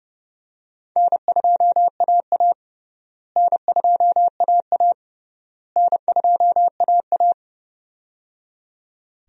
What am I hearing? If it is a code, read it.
Morse code, heard as a keyed tone, about 30 wpm, 700 Hz. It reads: N2AA N2AA N2AA
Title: USA Call Signs - 4 Characters - Repeated 3x - 30wpm